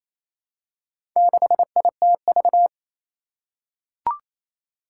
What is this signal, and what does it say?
Morse code, heard as a keyed tone, about 28 wpm, 700 Hz. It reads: 6ITV
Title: Four Challenging Letter-Number Combinations 28wpm